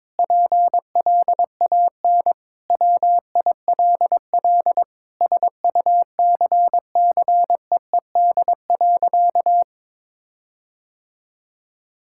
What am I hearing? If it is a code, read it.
Morse code, heard as a keyed tone, about 22 wpm, 700 Hz. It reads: PLAN WILL SUCCEED.
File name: Sentences with 9-Word Phrases - 22wpm